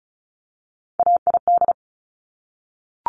Morse code, heard as a keyed tone, about 36 wpm, 700 Hz. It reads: AID
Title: Top Three Letter Combinations in Words 35wpm